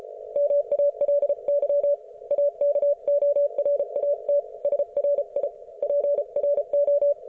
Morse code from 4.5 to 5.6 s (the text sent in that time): SRI